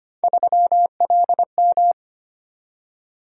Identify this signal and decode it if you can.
Morse code, heard as a keyed tone, about 25 wpm, 700 Hz. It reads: 3LM
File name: Three Letter-Number Combinations - 25wpm